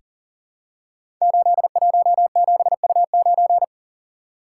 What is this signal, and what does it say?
Morse code, heard as a keyed tone, about 42 wpm, 690 Hz. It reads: 817U9
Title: Code Groups - 5 Letter:Numbers - 40wpm